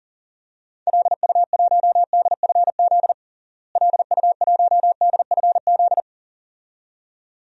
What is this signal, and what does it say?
Morse code, heard as a keyed tone, about 37 wpm, 700 Hz. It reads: LU1DFZ LU1DFZ
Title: DXCC-100 Call Signs - 6 Characters - Repeated 2x - 40wpm